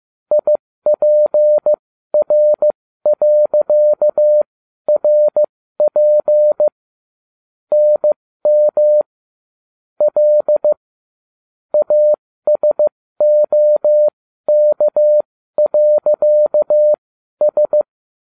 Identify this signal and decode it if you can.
Morse code, heard as a keyed tone, about 15 wpm, spaced as for 12 wpm, 600 Hz. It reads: IPR.RP NM L ASOK.S